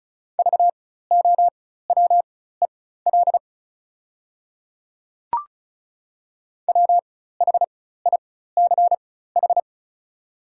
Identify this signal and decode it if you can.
Morse code, heard as a keyed tone, about 34 wpm, spaced as for 16 wpm, 700 Hz. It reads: VOWEL WHICH
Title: 4x Character Spacing - Top 500 Words - 35wpm